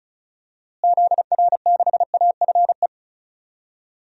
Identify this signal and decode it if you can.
Morse code, heard as a keyed tone, about 36 wpm, 700 Hz. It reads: ZR6AFE
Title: DXCC-100 Call Signs - 6 Characters - Repeated 2x - 35wpm